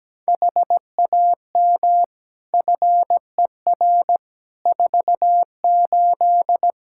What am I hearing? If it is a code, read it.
Morse code, heard as a keyed tone, about 17 wpm, 700 Hz. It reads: HAM FER 48
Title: K1USN SST QSO Practice 17wpm